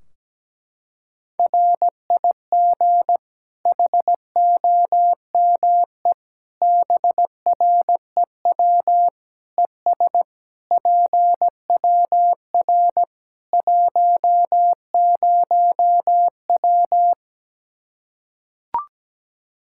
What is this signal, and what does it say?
Morse code, heard as a keyed tone, about 17 wpm, 700 Hz. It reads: RIG HOME BREW ES PWR 10W
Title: Practice QSO - Exchange 2 - Info - 17wpm